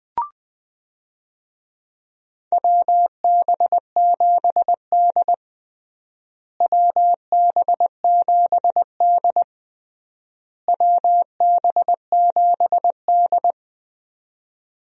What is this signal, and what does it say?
Morse code, heard as a keyed tone, about 20 wpm, 700 Hz. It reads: WB7D WB7D WB7D
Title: USA Call Signs - 4 Characters - Repeated 3x - 20wpm